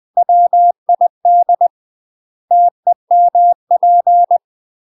Morse code, heard as a keyed tone, about 20 wpm, 700 Hz. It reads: WID TEMP
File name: UK MASTER QSOs TRAINING / 20WPM / UK MASTER QSOs 0044